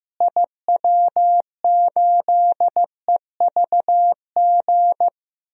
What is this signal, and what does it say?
Morse code, heard as a keyed tone, about 15 wpm, 700 Hz. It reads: IW8EVG
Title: DXCC-100 Call Signs - 6 Characters - Repeated 2x - 15wpm